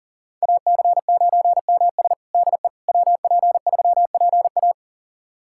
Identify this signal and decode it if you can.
Morse code, heard as a keyed tone, about 40 wpm, 700 Hz. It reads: AC9MS DE WP3PA
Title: USA Callsign DE Callsign - 5 Characters - 40wpm